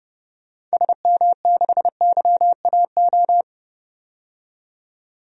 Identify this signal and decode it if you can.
Morse code, heard as a keyed tone, about 32 wpm, 700 Hz. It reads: SM6YAO
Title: DXCC-100 Call Signs - 6 Characters - 30wpm